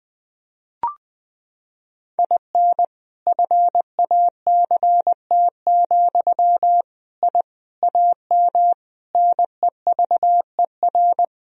Morse code, heard as a keyed tone, about 21 wpm, 700 Hz. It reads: IN FACT, I AM NEVER
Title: Sentences from Top 1000 Words - Review Part 3 - 20wpm